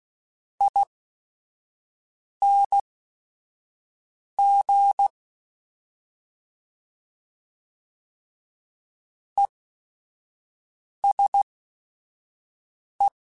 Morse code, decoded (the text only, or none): ING ESE